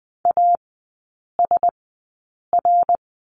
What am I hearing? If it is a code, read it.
Morse code, heard as a keyed tone, about 20 wpm, spaced as for 8 wpm, 700 Hz. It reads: ASR
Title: 053 Lesson 10e - Letter r - Three Characters 20wpm